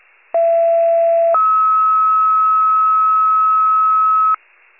Two-tone page, heard as a one-second tone, then a three-second tone, about 670 then 1290 Hz. It is heard from 0.3 to 4.4 s.
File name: wcalert